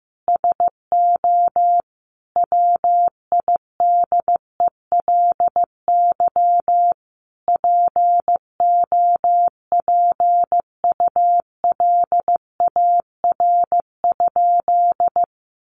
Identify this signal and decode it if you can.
Morse code, heard as a keyed tone, about 14 wpm, 700 Hz. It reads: SO WIDELY POPULAR?
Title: Sentences from Top 1200 Words 15wpm